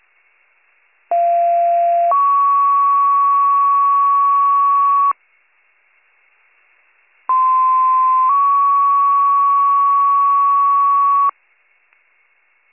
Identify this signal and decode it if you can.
Two-tone page, heard as a one-second tone, then a three-second tone, about 690 then 1120 Hz, about 1040 then 1120 Hz. It is heard from 1.1 to 11.3 s.